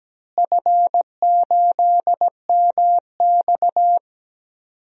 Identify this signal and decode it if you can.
Morse code, heard as a keyed tone, about 17 wpm, 700 Hz. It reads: F8MX
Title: DXCC-100 Call Signs - 4 Characters - Repeated 3x - 17wpm